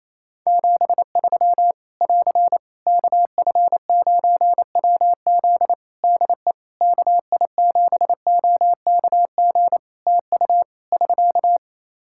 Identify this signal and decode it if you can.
Morse code, heard as a keyed tone, about 28 wpm, 700 Hz. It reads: 73 <AR> KF9WZ DE KI7OKG TU <SK>
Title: K1USN SST QSO Practice 28wpm